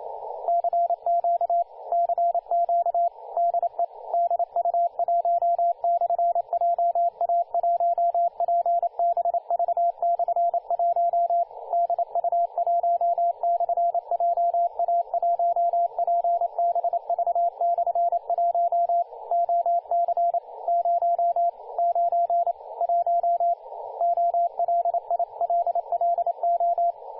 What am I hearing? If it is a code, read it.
Morse code, heard as a keyed tone, about 29 wpm, 690 Hz, with radio noise. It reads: CQ CQ DE DU1/JA1PBV/1 DU1/JA1PBV/1 OC 091 OLILLO
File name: du1-ja1pbv-1-28-040mhz-cw-cqing